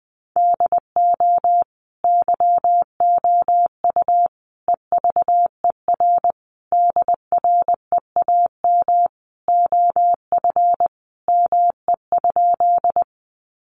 Morse code, heard as a keyed tone, about 20 wpm, 700 Hz. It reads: DO YOU EVER DREAM OF ME?
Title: Sentences from Top 1500 Words 20wpm